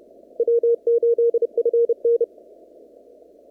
Morse code, heard as a keyed tone, about 30 wpm, 470 Hz. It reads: W8FN
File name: CW / kt5j wpx cw-op-k5pi